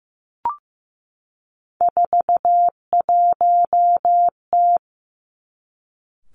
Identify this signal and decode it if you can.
Morse code, heard as a keyed tone, about 15 wpm, 700 Hz. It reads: 41T
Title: Three Challenging Letter-Number Combinations 15wpm